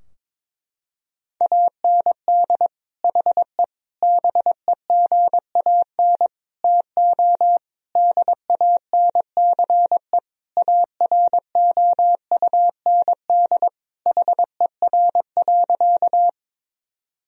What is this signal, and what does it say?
Morse code, heard as a keyed tone, about 22 wpm, 700 Hz. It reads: AND HE BEGAN TO DANCE AROUND HER.
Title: Sentences from Top 1000 Words - Review Part 4 - 22wpm